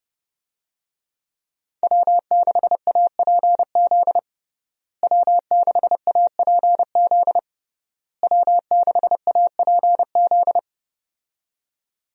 Morse code, heard as a keyed tone, about 32 wpm, 700 Hz. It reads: W6APZ W6APZ W6APZ
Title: USA Call Signs - 5 Characters - Repeated 3x - 30wpm